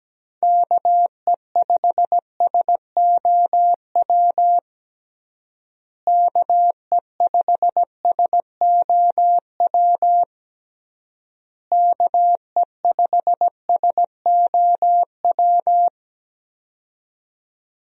Morse code, heard as a keyed tone, about 17 wpm, 700 Hz. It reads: KE5SOW KE5SOW KE5SOW
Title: USA Call Signs - 6 Characters - Repeated 3x - 17wpm